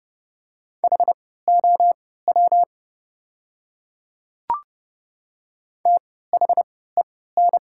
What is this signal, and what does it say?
Morse code, heard as a keyed tone, about 30 wpm, spaced as for 17 wpm, 700 Hz. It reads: HOW THEN
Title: 3x Character Spacing - Top 100 Words - 30wpm